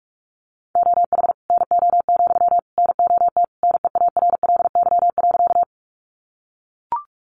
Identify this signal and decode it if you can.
Morse code, heard as a keyed tone, about 44 wpm, 700 Hz. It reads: OH NO, NOT NEARLY.
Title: Sentences from Top 1300 Words 45wpm